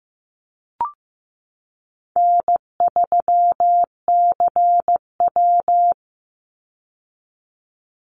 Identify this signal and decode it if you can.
Morse code, heard as a keyed tone, about 15 wpm, 700 Hz. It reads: N3CW
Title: Easy Call Signs Repeated 3x - 15wpm